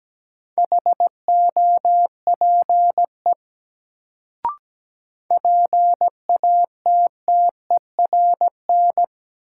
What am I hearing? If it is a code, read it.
Morse code, heard as a keyed tone, about 17 wpm, 700 Hz. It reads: HOPE PATTERN
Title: Sentences from Top 1000 Words - New Word Drill - 17wpm